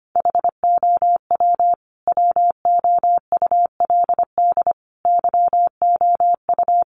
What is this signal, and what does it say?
Morse code, heard as a keyed tone, about 25 wpm, 700 Hz. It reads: HOW WOULD YOU
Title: Sentences from Top 100 Words 25wpm